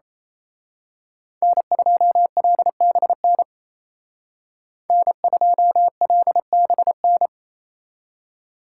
Morse code, heard as a keyed tone, about 29 wpm, 700 Hz. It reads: N2LBN N2LBN
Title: Speed Racing - USA 1x3 Call Signs 28wpm